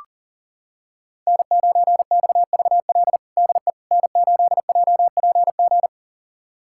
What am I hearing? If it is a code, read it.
Morse code, heard as a keyed tone, about 40 wpm, 700 Hz. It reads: N9XVL DE N8JPG
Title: USA Callsign DE Callsign - 5 Characters - 40wpm